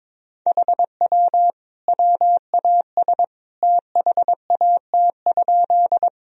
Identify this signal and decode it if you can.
Morse code, heard as a keyed tone, about 22 wpm, 700 Hz. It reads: HW WAS THAT?
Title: Sentences from Top 100 Words and Abbreviations 22wpm